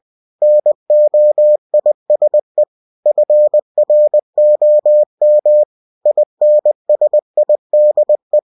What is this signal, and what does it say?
Morse code, heard as a keyed tone, about 20 wpm, 600 Hz. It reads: NOISE FROM INSIDE
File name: cw-books / winnie-the-pooh / winnie-20wpm-page-0013